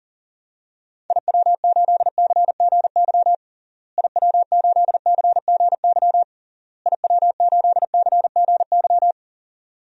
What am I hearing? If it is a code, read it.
Morse code, heard as a keyed tone, about 40 wpm, 700 Hz. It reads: IW8CGY IW8CGY IW8CGY
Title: DXCC-100 Call Signs - 6 Characters - Repeated 3x - 40wpm